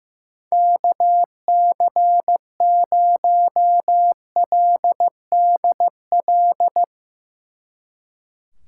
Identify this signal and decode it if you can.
Morse code, heard as a keyed tone, about 16 wpm, 700 Hz. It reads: KC0LDL
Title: Speed Racing - USA Call Signs 15wpm